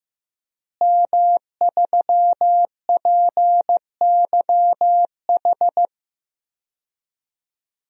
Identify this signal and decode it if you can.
Morse code, heard as a keyed tone, about 15 wpm, 700 Hz. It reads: M3PYH
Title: DXCC-100 Call Signs - 5 Characters - Repeated 3x - 15wpm